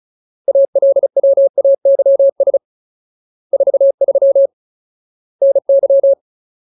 Morse code, heard as a keyed tone, about 36 wpm, 550 Hz. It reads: ALWAYS 43 NY
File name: Ultimate Word List / 35WPM Ultimate